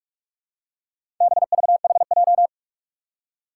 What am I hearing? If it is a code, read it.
Morse code, heard as a keyed tone, about 44 wpm, 700 Hz. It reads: BVHJ
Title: Four Challenging Letter-Number Combinations 45wpm